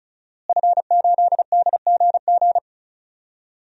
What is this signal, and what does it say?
Morse code, heard as a keyed tone, about 36 wpm, 700 Hz. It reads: F8DGG
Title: DXCC-100 Call Signs - 5 Characters - Repeated 3x - 35wpm